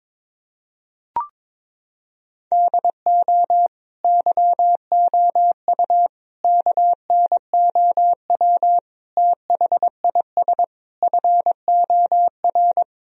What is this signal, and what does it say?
Morse code, heard as a keyed tone, about 22 wpm, 700 Hz. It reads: DO YOU KNOW THIS FOR
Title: Sentences from Top 1000 Words - Review Part 2 - 22wpm